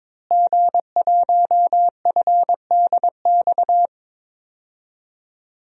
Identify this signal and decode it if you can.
Morse code, heard as a keyed tone, about 22 wpm, 700 Hz. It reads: G1FDX
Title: DXCC-100 Call Signs - 5 Characters - Repeated 3x - 22wpm